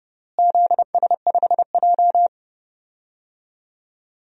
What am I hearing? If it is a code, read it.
Morse code, heard as a keyed tone, about 31 wpm, 700 Hz. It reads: ZS5J
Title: DXCC-100 Call Signs - 4 Characters - Repeated 3x - 30wpm